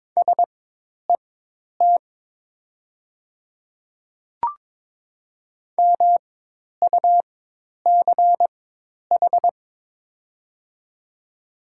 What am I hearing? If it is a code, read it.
Morse code, heard as a keyed tone, about 22 wpm, spaced as for 10 wpm, 700 Hz. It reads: SET MUCH
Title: 4x Character Spacing - Top 200 Words - 22wpm